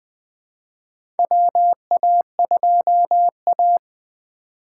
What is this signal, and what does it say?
Morse code, heard as a keyed tone, about 21 wpm, 700 Hz. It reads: WA2A
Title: USA Call Signs - 3x Character Spacing - 20wpm